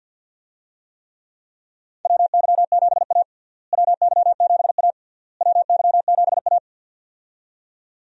Morse code, heard as a keyed tone, about 52 wpm, 700 Hz. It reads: WY7A WY7A WY7A
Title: USA Call Signs - 4 Characters - Repeated 3x - 50wpm